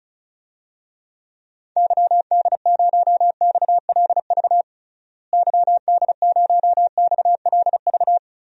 Morse code, heard as a keyed tone, about 34 wpm, 700 Hz. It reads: YD0XLV YD0XLV
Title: DXCC-100 Call Signs - 6 Characters - Repeated 3x - 35wpm